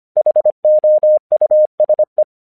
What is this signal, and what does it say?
Morse code, heard as a keyed tone, about 25 wpm, 610 Hz. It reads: HOUSE
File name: 1 English 25wpm / AQC10